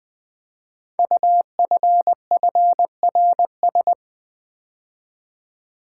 Morse code, heard as a keyed tone, about 20 wpm, 700 Hz. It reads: UFFRS
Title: Code Groups - 5 Letters - 20wpm